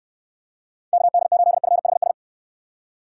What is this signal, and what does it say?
Morse code, heard as a keyed tone, about 69 wpm, 700 Hz. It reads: KD8FHS